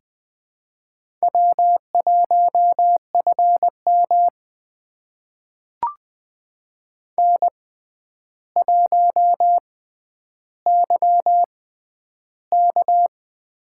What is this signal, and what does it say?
Morse code, heard as a keyed tone, about 20 wpm, spaced as for 10 wpm, 700 Hz. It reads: W1FM N1YK
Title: USA Call Signs - 4 Characters - 6x Character Spacing - 20wpm